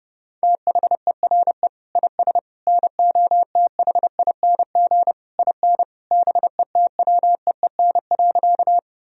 Morse code, heard as a keyed tone, about 30 wpm, 700 Hz. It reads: THERE IS NOTHING IN BETWEEN.
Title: Sentences from Top 1000 Words - Review Part 1 - 30wpm